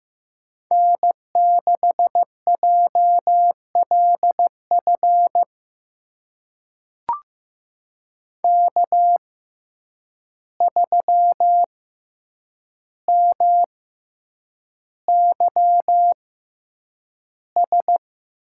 Morse code, heard as a keyed tone, about 15 wpm, spaced as for 8 wpm, 700 Hz. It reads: N6JLF K3MYS
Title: USA Call Signs - 5 Characters - 6x Character Spacing - 15wpm